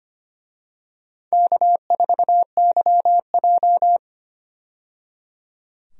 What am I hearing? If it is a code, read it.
Morse code, heard as a keyed tone, about 25 wpm, 700 Hz. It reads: K4YJ
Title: USA Call Signs - 4 Characters - 25wpm